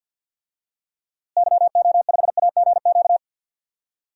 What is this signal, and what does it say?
Morse code, heard as a keyed tone, about 47 wpm, 700 Hz. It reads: YO5AGQ
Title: DXCC-100 Call Signs - 6 Characters - Repeated 2x - 50wpm